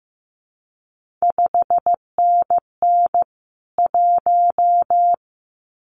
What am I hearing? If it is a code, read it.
Morse code, heard as a keyed tone, about 15 wpm, 700 Hz. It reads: 5NN 1
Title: CQ WW DX Contest Exchange Encore - 15wpm